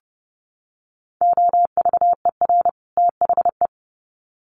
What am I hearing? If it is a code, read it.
Morse code, heard as a keyed tone, about 30 wpm, 700 Hz. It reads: OVER THE
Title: Sets of 2 Words 30wpm